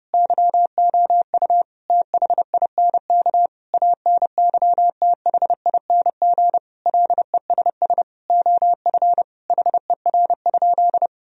Morse code, heard as a keyed tone, about 30 wpm, 700 Hz. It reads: YOU THINK ANYTHING LESS OF HER?
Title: Sentences from Top 1000 Words - Review Part 2 - 30wpm